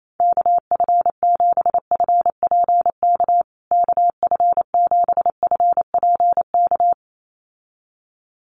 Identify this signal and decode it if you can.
Morse code, heard as a keyed tone, about 28 wpm, 700 Hz. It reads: KF7FPK KF7FPK
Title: USA Call Signs Repeated 2x 28wpm